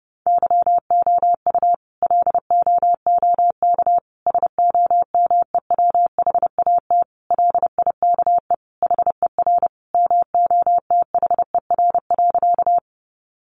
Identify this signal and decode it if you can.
Morse code, heard as a keyed tone, about 30 wpm, 700 Hz. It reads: YOU LOOK SOMEWHAT LIKE HER MOTHER.
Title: Sentences from Top 1300 Words 30wpm